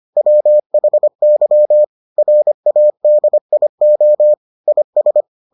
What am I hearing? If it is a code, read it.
Morse code, heard as a keyed tone, about 25 wpm, 600 Hz. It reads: WHY RADIO IS